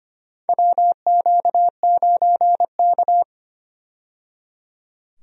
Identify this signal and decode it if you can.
Morse code, heard as a keyed tone, about 25 wpm, 700 Hz. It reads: WQ9K
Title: USA Call Signs - 4 Characters - 25wpm